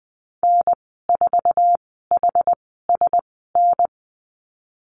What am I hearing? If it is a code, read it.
Morse code, heard as a keyed tone, about 20 wpm, spaced as for 15 wpm, 700 Hz. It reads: N4HSN
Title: 076 Lesson 13g - Letter L - Call Signs 20wpm